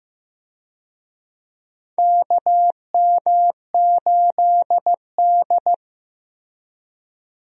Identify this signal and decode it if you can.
Morse code, heard as a keyed tone, about 15 wpm, 700 Hz. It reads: KM8D
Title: USA Call Signs - 4 Characters - Repeated 2x - 15wpm